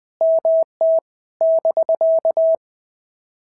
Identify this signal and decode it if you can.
Morse code, heard as a keyed tone, about 20 wpm, 650 Hz. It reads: MT <BK>
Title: POTA QSO Practice - 20wpm